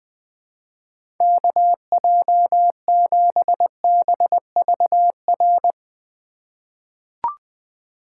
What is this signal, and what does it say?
Morse code, heard as a keyed tone, about 20 wpm, 700 Hz. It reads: KJ7BVR
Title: USA Call Signs - 6 Characters - 2x Character Spacing - 20wpm